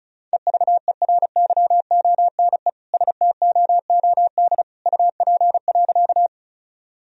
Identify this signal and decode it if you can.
Morse code, heard as a keyed tone, about 34 wpm, 700 Hz. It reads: EVERYONE STOOD UP.
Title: Sentences from Top 1000 Words - Review Part 2 - 35wpm